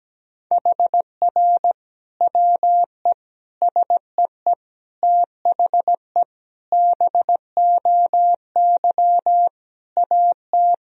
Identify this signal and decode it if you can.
Morse code, heard as a keyed tone, about 17 wpm, 700 Hz. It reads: HR WE SEE THE BOY AT
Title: Sentences from Top 300 Words and Abbreviations 17wpm